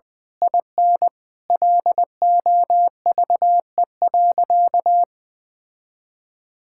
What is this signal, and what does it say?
Morse code, heard as a keyed tone, about 20 wpm, 700 Hz. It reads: IN LOVE.
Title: Sentences from Top 1600 Words 20wpm